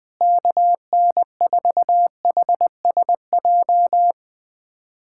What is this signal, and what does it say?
Morse code, heard as a keyed tone, about 20 wpm, 700 Hz. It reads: KN4HSJ
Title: USA Callsign DE Callsign 20wpm